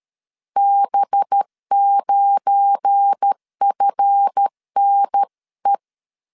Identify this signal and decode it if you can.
Morse code, heard as a keyed tone, about 13 wpm, 800 Hz. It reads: B9FNE
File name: LEZIONI / 8 YJ45 / 8 Y 12wpm 2228